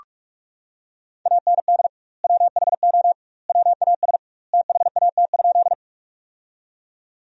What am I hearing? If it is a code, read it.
Morse code, heard as a keyed tone, about 46 wpm, 700 Hz. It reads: AND WHO WAS THAT?
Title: 2x Word-Spacing - Sentences from Top 100 Words - 45wpm